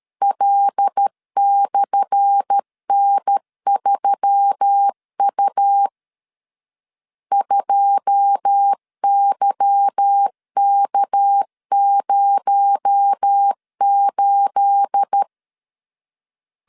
Morse code, heard as a keyed tone, about 13 wpm, 800 Hz. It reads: L/N3U 2YK08
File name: LEZIONI / 10 SIGNS / 10A SLASH 12wpm 2314